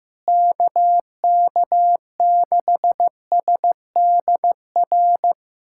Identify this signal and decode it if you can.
Morse code, heard as a keyed tone, about 16 wpm, 700 Hz. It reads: KK6SDR
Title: USA Callsign DE Callsign - 6 Characters - 15wpm